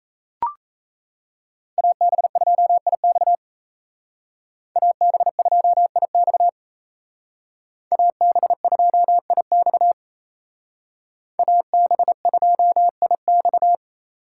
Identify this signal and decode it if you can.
Morse code, heard as a keyed tone, about 33 wpm, 700 Hz. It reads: AB2IX AB2IX AB2IX AB2IX
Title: Speed Racing - USA 2x2 Call Signs 28wpm